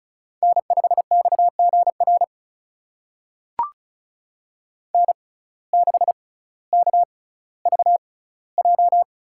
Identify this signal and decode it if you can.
Morse code, heard as a keyed tone, about 34 wpm, spaced as for 18 wpm, 700 Hz. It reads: N5XGR N6KVJ
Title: USA Call Signs - 5 Characters - 6x Character Spacing - 35wpm